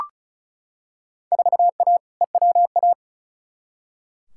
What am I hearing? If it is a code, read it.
Morse code, heard as a keyed tone, about 34 wpm, 700 Hz. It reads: 4A EWA
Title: Field Day Exchanges - 35wpm